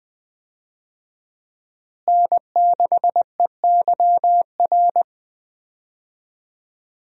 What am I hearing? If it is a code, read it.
Morse code, heard as a keyed tone, about 21 wpm, 700 Hz. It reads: N6EYR
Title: USA Call Signs - 5 Characters - Repeated 3x - 20wpm